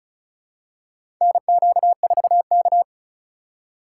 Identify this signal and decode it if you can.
Morse code, heard as a keyed tone, about 36 wpm, 700 Hz. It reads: NQ4K
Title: USA Call Signs - 4 Characters - 4x Character Spacing - 35wpm